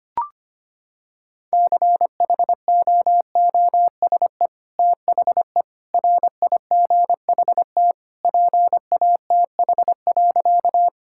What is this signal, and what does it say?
Morse code, heard as a keyed tone, about 25 wpm, 700 Hz. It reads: CHOOSE THE RIGHT PATH.
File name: practice / 25wpm / Sentences with 2-Word Phrases - 25wpm